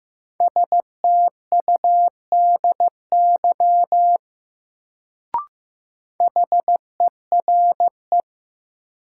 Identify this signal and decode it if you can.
Morse code, heard as a keyed tone, about 15 wpm, 700 Hz. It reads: STUDY HERE
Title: Sets of 5 Words - New Word Drill - 15wpm